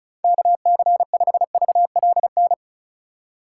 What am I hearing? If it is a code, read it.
Morse code, heard as a keyed tone, about 34 wpm, 700 Hz. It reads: KC5VLN
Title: USA Call Signs - 2x Character Spacing - 35wpm